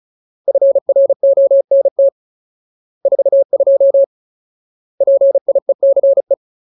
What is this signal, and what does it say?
Morse code, heard as a keyed tone, about 35 wpm, 550 Hz. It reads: FRONT 42 PIECE
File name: Ultimate Word List / 35WPM Ultimate